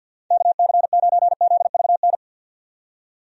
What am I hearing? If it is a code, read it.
Morse code, heard as a keyed tone, about 49 wpm, 700 Hz. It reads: KC9ZVN
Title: USA Call Signs - 6 Characters - 50wpm